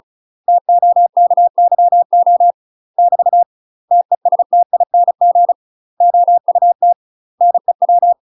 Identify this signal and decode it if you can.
Morse code, heard as a keyed tone, about 34 wpm, 700 Hz. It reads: TOKYO <BT> TESTING OUT NEW
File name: UK MASTER QSOs TRAINING / 35WPM / UK MASTER QSOs 0004